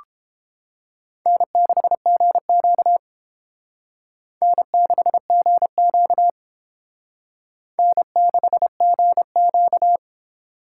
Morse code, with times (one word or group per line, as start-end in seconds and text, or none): 1.3-3.0 N6GQ
4.4-6.3 N6GQ
7.8-10.0 N6GQ